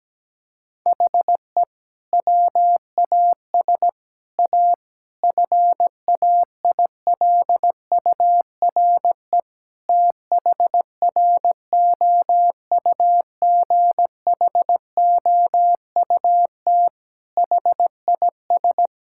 Morse code, heard as a keyed tone, about 17 wpm, 700 Hz. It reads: HE WAS A FAILURE THROUGHOUT HIS
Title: Sentences from Top 1000 Words - Review Part 6 - 17wpm